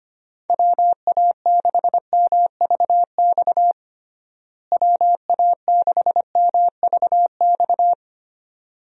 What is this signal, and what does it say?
Morse code, heard as a keyed tone, about 25 wpm, 700 Hz. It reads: WA6MVX WA6MVX
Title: USA Call Signs - 6 Characters - Repeated 2x - 25wpm